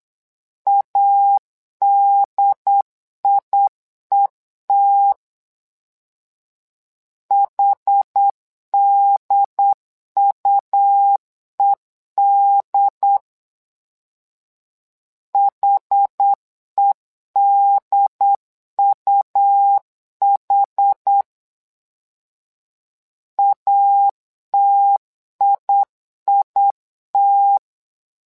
Morse code, heard as a keyed tone, about 8 wpm, 800 Hz. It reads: ADIET HDUED HEDUH ATIIT
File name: LEZIONI / 2 UHDN / 2 UHD 8wpm 2258